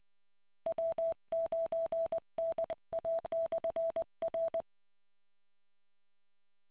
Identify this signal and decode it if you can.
Morse code, heard as a keyed tone, about 20 wpm, 660 Hz. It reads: W9NA/R